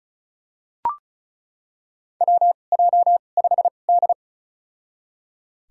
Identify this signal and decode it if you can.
Morse code, heard as a keyed tone, about 36 wpm, spaced as for 25 wpm, 700 Hz. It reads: WJ5D
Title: USA Call Signs - 2x Character Spacing - 35wpm